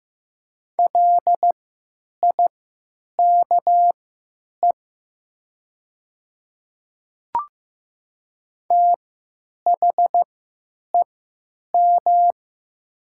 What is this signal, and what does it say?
Morse code, heard as a keyed tone, about 15 wpm, spaced as for 9 wpm, 700 Hz. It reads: LIKE THEM
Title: 3x Character Spacing - Top 100 Words - 15wpm